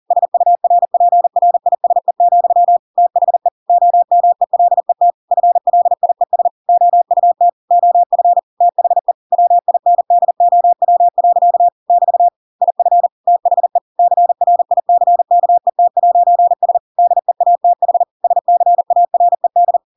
Morse code, heard as a keyed tone, about 40 wpm, 700 Hz. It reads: SURPRISE, THE OMELET FLIES OUT OF THE WINDOW. <BT> IF THE CRICKET'S DEATH SCARED
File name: Pinocchio / Pinocchio ch5 40 40 ik1hge